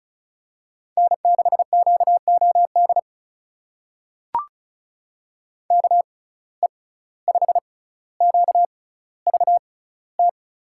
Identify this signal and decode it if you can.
Morse code, heard as a keyed tone, about 36 wpm, spaced as for 17 wpm, 700 Hz. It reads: N6QOD KE5QVT
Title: USA Call Signs - 6x Character Spacing - 35wpm